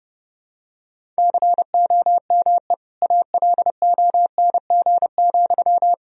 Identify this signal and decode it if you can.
Morse code, heard as a keyed tone, about 30 wpm, 700 Hz. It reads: COME ALONG,
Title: Sentences from Top 1000 Words - Review Part 1 - 30wpm